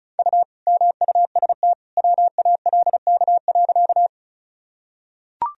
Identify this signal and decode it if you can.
Morse code, heard as a keyed tone, about 36 wpm, 700 Hz. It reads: U MUST WALK.
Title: Sentences from Top 300 Words and Abbreviations 35wpm